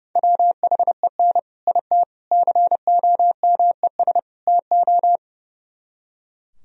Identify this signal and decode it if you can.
Morse code, heard as a keyed tone, about 38 wpm, 700 Hz. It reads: WHEN IT COMES TO